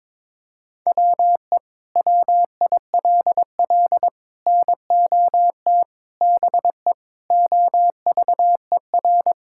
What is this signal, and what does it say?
Morse code, heard as a keyed tone, about 22 wpm, 700 Hz. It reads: WE WILL NOT BE OVER
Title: Sentences from Top 1000 Words - Review Part 5 - 22wpm